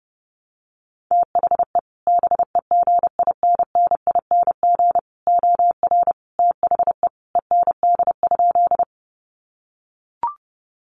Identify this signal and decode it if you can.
Morse code, heard as a keyed tone, about 30 wpm, 700 Hz. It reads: THE BEGINNING OR THE END?
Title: Sentences from Top 800 Words 30wpm